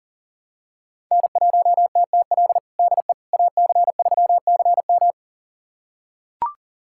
Morse code, heard as a keyed tone, about 40 wpm, 700 Hz. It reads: N1TTL DE AC3CM
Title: USA Callsign DE Callsign - 5 Characters - 40wpm